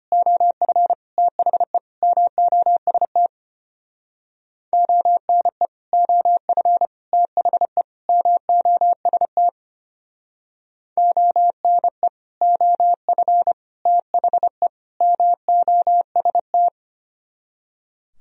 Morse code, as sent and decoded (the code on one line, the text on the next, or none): --- ..-. / - .... / -- --- ... - / --- -. / --- ..-. / - .... / -- --- ... - / --- -. / --- ..-. / - .... / -- --- ... -
OF THE MOST ONE OF THE MOST ONE OF THE MOST